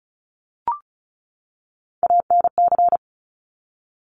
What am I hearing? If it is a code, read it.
Morse code, heard as a keyed tone, about 36 wpm, 700 Hz. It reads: ANC
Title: Top Three Letter Combinations in Words 35wpm